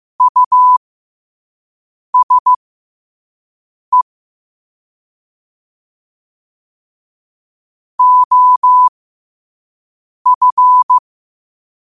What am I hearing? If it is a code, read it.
Morse code, heard as a keyed tone, about 15 wpm, spaced as for 5 wpm, 1000 Hz. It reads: USE OF
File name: 5wpm